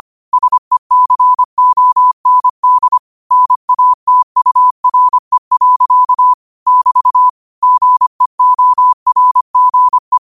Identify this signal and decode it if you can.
Morse code, heard as a keyed tone, about 25 wpm, 1000 Hz. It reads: SECOND NATURE. <BT> GEORGE
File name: qotdfiles / 25wpm